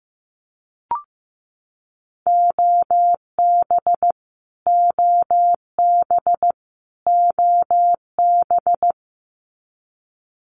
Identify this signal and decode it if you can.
Morse code, heard as a keyed tone, about 15 wpm, 700 Hz. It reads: OB OB OB
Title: Abbreviations Repeated 3x - 15wpm